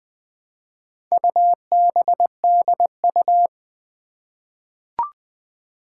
Morse code, heard as a keyed tone, about 20 wpm, 700 Hz. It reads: UBDU
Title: Four Challenging Letter-Number Combinations 20wpm